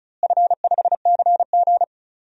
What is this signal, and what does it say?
Morse code, heard as a keyed tone, about 36 wpm, 690 Hz. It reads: F5CG
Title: DXCC-100 Call Signs - 4 Characters - Repeated 3x - 35wpm